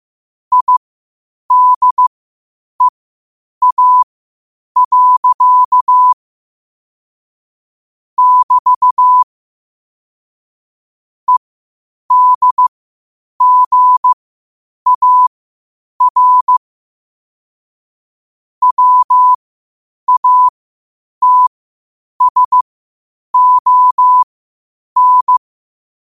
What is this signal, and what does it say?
Morse code, heard as a keyed tone, about 15 wpm, spaced as for 8 wpm, 1000 Hz. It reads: IDEA. <BT> EDGAR WATSON
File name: qotdfiles / 8wpm